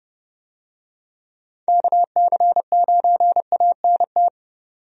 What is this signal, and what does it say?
Morse code, heard as a keyed tone, about 31 wpm, 700 Hz. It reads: KC9ANT
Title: USA Call Signs - 6 Characters - 30wpm